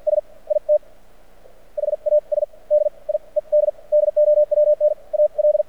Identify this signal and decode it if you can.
Morse code, heard as a keyed tone, about 50 wpm, spaced as for 39 wpm, 600 Hz. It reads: S IT HAS DIED DOWN AL
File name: archived / CRSnet